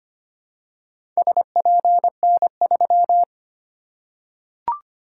Morse code, heard as a keyed tone, about 25 wpm, 700 Hz. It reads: SPN3
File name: practice / 25wpm / Four Letter-Number Combinations - 25wpm